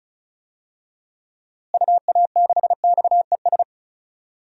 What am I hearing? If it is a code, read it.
Morse code, heard as a keyed tone, about 36 wpm, 700 Hz. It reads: UA6XES
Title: DXCC-100 Call Signs - 6 Characters - Repeated 2x - 35wpm